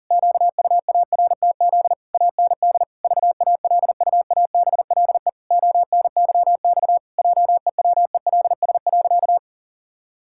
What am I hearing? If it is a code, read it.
Morse code, heard as a keyed tone, about 40 wpm, 700 Hz. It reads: QUARTZ AND VALUABLE ONYX JEWELS.
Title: Pangram Challenge v2 - 40wpm